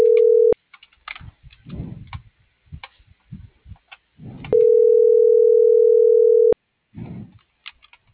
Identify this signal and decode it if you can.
Ringback tone (North America), 440 + 480 Hz.
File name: CENTRAL MAERSK / out adolfo.msk